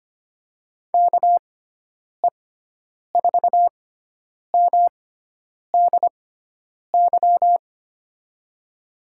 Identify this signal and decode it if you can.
Morse code, heard as a keyed tone, about 25 wpm, spaced as for 9 wpm, 700 Hz. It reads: KE4MDY
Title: USA Call Signs - 6 Characters - 6x Character Spacing - 25wpm